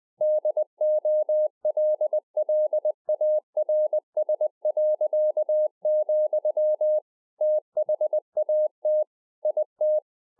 Morse code, heard as a keyed tone, about 20 wpm, 610 Hz. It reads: DOLLARS., THAT IT
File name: 2011-04-mp3 / A SMALL TRANCEIVER 3 20WPM